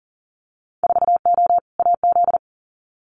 Morse code, heard as a keyed tone, about 42 wpm, 700 Hz. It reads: 4O AZ
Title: Winter Field Day Exchanges Encore 40wpm